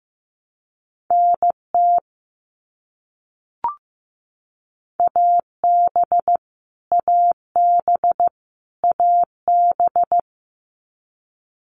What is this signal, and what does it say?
Morse code, heard as a keyed tone, about 15 wpm, 700 Hz. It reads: NT AB AB AB
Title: Canadian Provinces and Territories - Repeated 3x - 15wpm